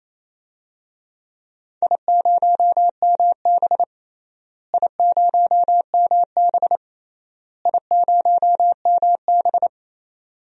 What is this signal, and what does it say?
Morse code, heard as a keyed tone, about 27 wpm, 700 Hz. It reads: I0MB I0MB I0MB
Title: DXCC-100 Call Signs - 4 Characters - Repeated 3x - 28wpm